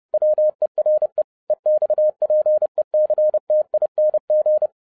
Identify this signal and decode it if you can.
Morse code, heard as a keyed tone, about 30 wpm, 610 Hz. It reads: WERE EXPECTING
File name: 30wpm